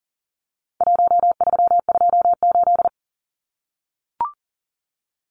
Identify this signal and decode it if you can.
Morse code, heard as a keyed tone, about 40 wpm, 700 Hz. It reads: 1328
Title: Numbers - 4-Digit 40wpm